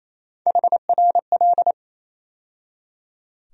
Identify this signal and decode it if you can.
Morse code, heard as a keyed tone, about 28 wpm, 700 Hz. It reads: HRL
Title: USA Call Sign Postfixes - 3 Letters 28wpm